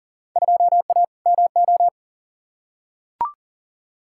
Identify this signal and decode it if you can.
Morse code, heard as a keyed tone, about 39 wpm, 700 Hz. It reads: 2A MO
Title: Field Day Exchanges Encore - 40wpm